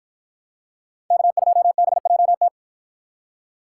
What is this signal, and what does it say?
Morse code, heard as a keyed tone, about 53 wpm, 700 Hz. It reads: K2BJT